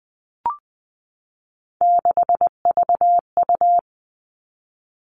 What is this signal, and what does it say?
Morse code, heard as a keyed tone, about 20 wpm, 700 Hz. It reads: 6VU
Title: All Cut Numbers - 3-Digit 20wpm